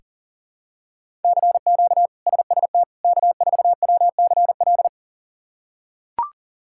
Morse code, heard as a keyed tone, about 40 wpm, 700 Hz. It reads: CQ SST K4WCL
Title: K1USN SST QSO Practice - 40wpm